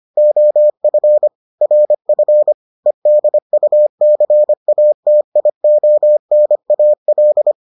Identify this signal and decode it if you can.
Morse code, heard as a keyed tone, about 25 wpm, 600 Hz. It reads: OF RF EDUCATIONAL